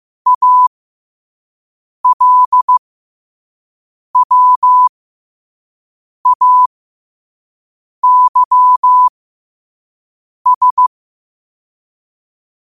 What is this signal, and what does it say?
Morse code, heard as a keyed tone, about 15 wpm, spaced as for 5 wpm, 1000 Hz. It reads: ALWAYS
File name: qotdfiles / 5wpm